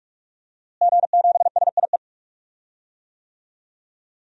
Morse code, heard as a keyed tone, about 46 wpm, 700 Hz. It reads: G7SIE
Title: DXCC-100 Call Signs - 5 Characters - Repeated 2x - 45wpm